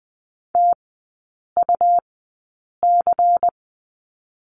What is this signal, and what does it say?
Morse code, heard as a keyed tone, about 20 wpm, spaced as for 8 wpm, 700 Hz. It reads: TUC
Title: 138 Lesson 23e - Letter f - Three Characters 20wpm